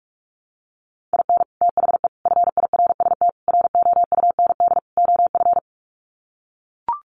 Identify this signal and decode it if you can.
Morse code, heard as a keyed tone, about 44 wpm, 700 Hz. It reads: IN THE FIRST ROUND OF
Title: Sets of 5 Words Encore 45wpm